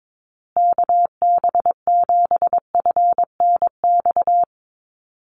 Morse code, heard as a keyed tone, about 22 wpm, 700 Hz. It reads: KB7FNX
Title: USA Call Signs Repeated 2x 22wpm